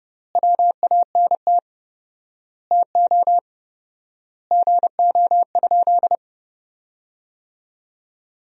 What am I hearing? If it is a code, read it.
Morse code, heard as a keyed tone, about 30 wpm, 700 Hz. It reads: WANT TO GO?
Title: 4x Word-Spacing - Sentences from Top 300 Words - 30wpm